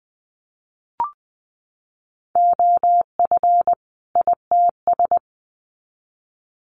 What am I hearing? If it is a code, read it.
Morse code, heard as a keyed tone, about 20 wpm, 700 Hz. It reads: OF ITS
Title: Sets of 2 Words 20wpm